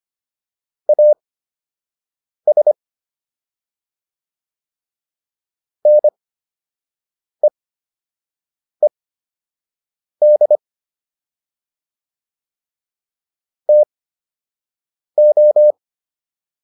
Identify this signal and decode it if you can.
Morse code, heard as a keyed tone, about 25 wpm, spaced as for 6 wpm, 600 Hz. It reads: AS NEED TO